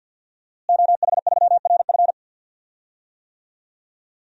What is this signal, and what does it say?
Morse code, heard as a keyed tone, about 52 wpm, 700 Hz. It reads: OH3RF
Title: DXCC-100 Call Signs - 5 Characters - Repeated 2x - 50wpm